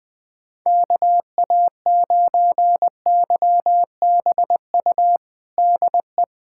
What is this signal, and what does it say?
Morse code, heard as a keyed tone, about 20 wpm, 700 Hz. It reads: KA9YBU DE
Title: USA Callsign DE Callsign 20wpm